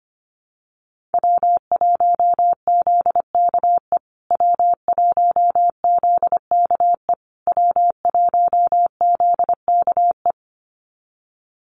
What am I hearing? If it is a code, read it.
Morse code, heard as a keyed tone, about 25 wpm, 700 Hz. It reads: W1ZKE W1ZKE W1ZKE
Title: USA Call Signs Repeated 3x 25wpm